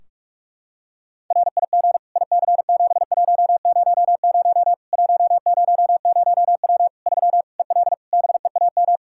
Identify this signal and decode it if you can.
Morse code, heard as a keyed tone, about 42 wpm, 700 Hz. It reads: RIG IC7100 100W 3 EL BEAM
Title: Practice QSO - Exchange 2 - Info - 45wpm